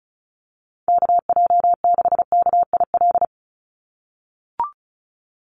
Morse code, heard as a keyed tone, about 36 wpm, 700 Hz. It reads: KJ6KIL
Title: USA Call Signs 35wpm